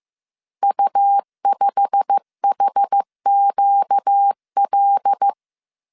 Morse code, heard as a keyed tone, about 15 wpm, 800 Hz. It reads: U5HQL
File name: CW / DET / 1 9 1500C S A 14wpm 3222